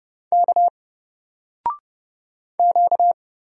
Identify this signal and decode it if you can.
Morse code, heard as a keyed tone, about 30 wpm, 700 Hz. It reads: K Q
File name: Single-Letter-Number-30wpm